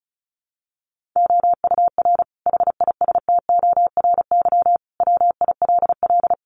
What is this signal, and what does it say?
Morse code, heard as a keyed tone, about 34 wpm, 700 Hz. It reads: OUR HISTORY WILL
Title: Sentences from Top 1500 Words 35wpm